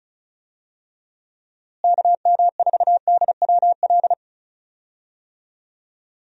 Morse code, heard as a keyed tone, about 34 wpm, 700 Hz. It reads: KM4DWL
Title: USA Call Signs - 6 Characters - Repeated 2x - 35wpm